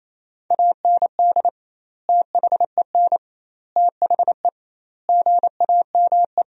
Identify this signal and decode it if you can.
Morse code, heard as a keyed tone, about 28 wpm, 700 Hz. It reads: AND THEN THE GAME
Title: 2x Word-Spacing - Sentences from Top 500 Words - 28wpm